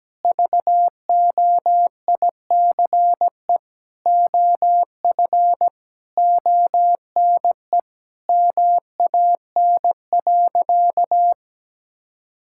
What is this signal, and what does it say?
Morse code, heard as a keyed tone, about 17 wpm, 700 Hz. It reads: VOICE OF ONE MAN.
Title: Sentences from Top 1000 Words - Review Part 6 - 17wpm